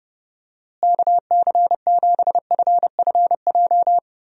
Morse code, heard as a keyed tone, about 31 wpm, 700 Hz. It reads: KC7FFJ
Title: USA Call Signs - 6 Characters - Repeated 2x - 30wpm